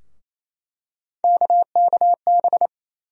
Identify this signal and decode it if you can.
Morse code, heard as a keyed tone, about 28 wpm, 700 Hz. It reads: KKB
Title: USA Call Sign Postfixes - 3 Letters 28wpm